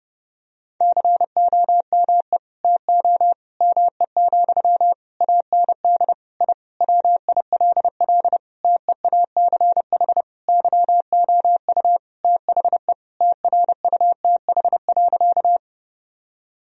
Morse code, heard as a keyed tone, about 32 wpm, 700 Hz. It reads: COME TO ME, AND I WILL TEACH YOU THE TRUTH.
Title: Sentences from Top 1000 Words - Review Part 2 - 30wpm